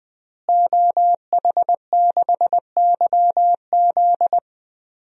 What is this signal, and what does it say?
Morse code, heard as a keyed tone, about 20 wpm, 700 Hz. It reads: OH6YZ
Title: DXCC-100 Call Signs - 5 Characters - Repeated 2x - 20wpm